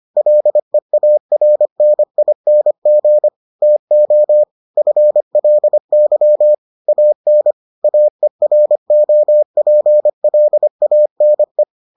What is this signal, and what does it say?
Morse code, heard as a keyed tone, about 25 wpm, 600 Hz. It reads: LEARNING TO FLY AN AEROPLANE